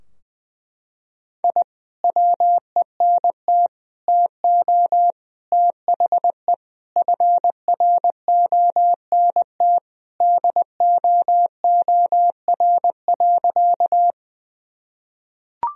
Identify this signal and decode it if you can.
Morse code, heard as a keyed tone, about 20 wpm, 700 Hz. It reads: I WENT TO THE FRONT DOOR.
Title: Sentences from Top 1000 Words - Review Part 2 - 20wpm